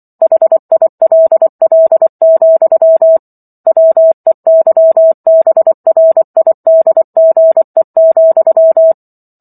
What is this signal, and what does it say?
Morse code, heard as a keyed tone, about 24 wpm, 650 Hz. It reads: HILL, WEYBRIDGE,